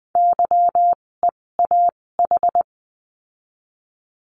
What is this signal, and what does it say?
Morse code, heard as a keyed tone, about 19 wpm, spaced as for 16 wpm, 700 Hz. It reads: YEAH
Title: 256 Lesson 42a - Words 12-20wpm